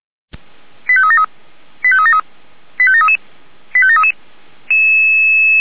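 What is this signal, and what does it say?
Five-tone selective call (ZVEI), heard as quick runs of five tones, tones of about 70 ms. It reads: 85373 85373 85733 85733
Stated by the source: ZVEI five-tone call received by SDR.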